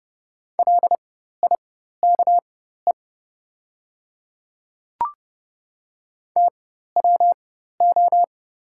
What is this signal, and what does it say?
Morse code, heard as a keyed tone, about 30 wpm, spaced as for 14 wpm, 700 Hz. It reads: LIKE TWO
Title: 4x Character Spacing - Top 100 Words - 30wpm